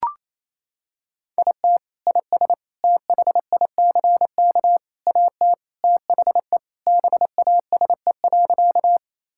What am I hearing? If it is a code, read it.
Morse code, heard as a keyed tone, about 28 wpm, 700 Hz. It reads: IT IS THICK AT THE BASE.
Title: Sentences from Top 1600 Words 28wpm